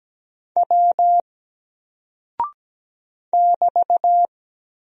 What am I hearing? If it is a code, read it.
Morse code, heard as a keyed tone, about 17 wpm, 700 Hz. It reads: W <BT>
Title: Single Character (Letter:Number:Punctuation:Prosign) Alt 17wpm